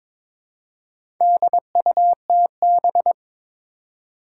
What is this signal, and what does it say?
Morse code, heard as a keyed tone, about 22 wpm, 700 Hz. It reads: DUTB
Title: Four Challenging Letter-Number Combinations 22wpm